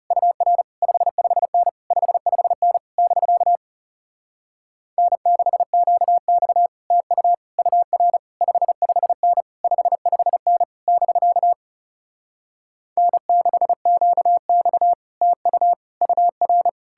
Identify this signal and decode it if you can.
Morse code, heard as a keyed tone, about 40 wpm, 700 Hz. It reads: UR 55N 55N <BK> N6QX TU UR 55N 55N <BK> N6QX TU UR